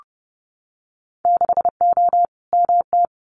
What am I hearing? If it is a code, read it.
Morse code, heard as a keyed tone, about 30 wpm, 700 Hz. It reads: 6O MT
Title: Winter Field Day Exchanges Encore 30wpm